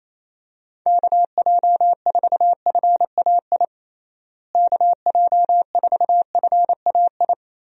Morse code, heard as a keyed tone, about 27 wpm, 700 Hz. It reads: KJ4FAI KJ4FAI
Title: USA Call Signs - 6 Characters - Repeated 3x - 28wpm